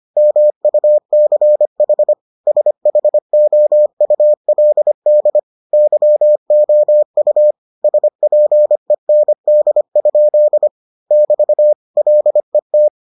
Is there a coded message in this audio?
Yes: Morse code from 0.2 to 12.9 s, heard as a keyed tone, about 25 wpm, 600 Hz. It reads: MUCH SHOULD YOU SPEND? <BT> LET